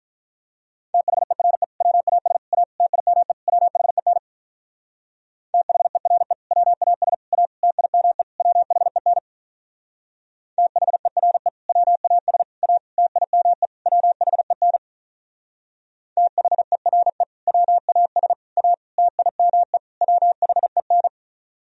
Morse code, heard as a keyed tone, about 43 wpm, 700 Hz. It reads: THERE WAS A TIME WHEN THERE WAS A TIME WHEN THERE WAS A TIME WHEN THERE WAS A TIME WHEN
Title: Speed-Racing - Sets of 5 Words 35wpm